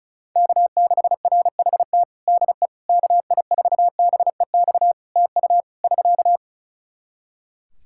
Morse code, heard as a keyed tone, about 36 wpm, 700 Hz. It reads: K6RHT DE KI4BEX TU <SK>
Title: K1USN SST QSO Practice 35wpm